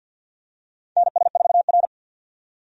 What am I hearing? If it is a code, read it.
Morse code, heard as a keyed tone, about 50 wpm, 700 Hz. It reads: NS4R
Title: USA Call Signs - 4 Characters - 50wpm